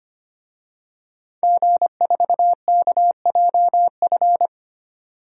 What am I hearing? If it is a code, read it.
Morse code, heard as a keyed tone, about 25 wpm, 700 Hz. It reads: G4KJF
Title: DXCC-100 Call Signs - 5 Characters - 25wpm